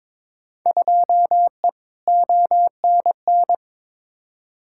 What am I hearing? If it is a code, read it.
Morse code, heard as a keyed tone, about 22 wpm, 700 Hz. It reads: 2E ONN
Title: Field Day Exchanges Encore - 22wpm